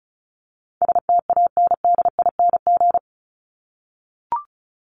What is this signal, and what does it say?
Morse code, heard as a keyed tone, about 34 wpm, 700 Hz. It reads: STANDING
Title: ING Suffix - Top 100 Words with Base Words 35wpm